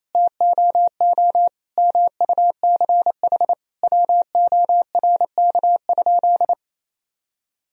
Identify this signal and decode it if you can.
Morse code, heard as a keyed tone, about 27 wpm, 700 Hz. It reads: TOO MUCH WORK?
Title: Sentences from Top 1000 Words - Review Part 1 - 28wpm